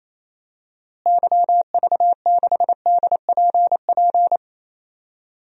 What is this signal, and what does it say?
Morse code, heard as a keyed tone, about 28 wpm, 700 Hz. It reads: YV6DPP
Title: DXCC-100 Call Signs - 6 Characters - Repeated 2x - 28wpm